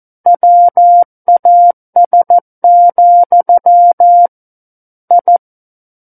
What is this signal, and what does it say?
Morse code, heard as a keyed tone, about 14 wpm, 700 Hz. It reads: WAS, I
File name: Book One / 05-Chapter 5 - 14 WPM